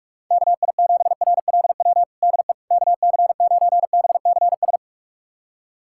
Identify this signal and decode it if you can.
Morse code, heard as a keyed tone, about 44 wpm, 700 Hz. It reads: KI7RLW DE KC9BCS
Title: USA Callsign DE Callsign - 6 Characters - 45wpm